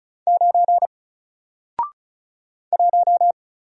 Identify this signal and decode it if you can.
Morse code, heard as a keyed tone, about 36 wpm, 700 Hz. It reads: 9 1
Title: Single Letter-Number Alt 35wpm